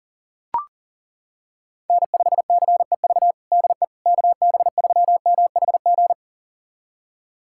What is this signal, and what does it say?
Morse code, heard as a keyed tone, about 40 wpm, 700 Hz. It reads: N5CEV DE KB3MHG
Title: USA Callsign DE Callsign 40wpm